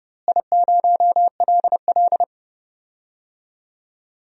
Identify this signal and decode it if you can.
Morse code, heard as a keyed tone, about 30 wpm, 700 Hz. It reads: I0LL
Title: DXCC-100 Call Signs - 4 Characters - Repeated 2x - 30wpm